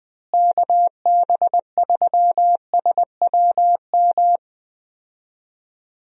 Morse code, heard as a keyed tone, about 20 wpm, 700 Hz. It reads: KB3SWM
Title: USA Call Signs - 6 Characters - Repeated 3x - 20wpm